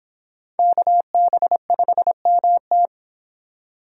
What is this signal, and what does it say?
Morse code, heard as a keyed tone, about 26 wpm, 700 Hz. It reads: KB5MT